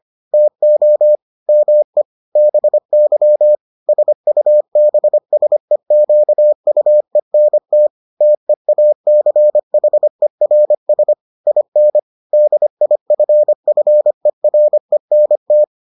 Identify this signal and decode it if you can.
Morse code, heard as a keyed tone, about 25 wpm, 600 Hz. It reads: TO ME BY SUBSEQUENT TEACHERS IN DIFFERENT